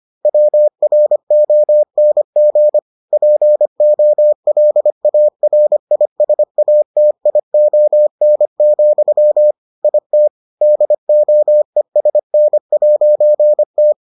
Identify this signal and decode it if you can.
Morse code, heard as a keyed tone, about 25 wpm, 600 Hz. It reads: WRONG POLARISATION, IT DOESN'T